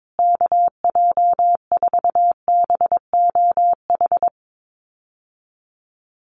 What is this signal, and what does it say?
Morse code, heard as a keyed tone, about 22 wpm, 700 Hz. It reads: KJ4BOH
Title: USA Call Signs Repeated 2x 22wpm